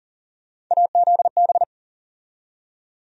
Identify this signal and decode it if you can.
Morse code, heard as a keyed tone, about 40 wpm, 700 Hz. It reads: AZB
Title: USA Call Sign Postfixes - 3 Letters 40wpm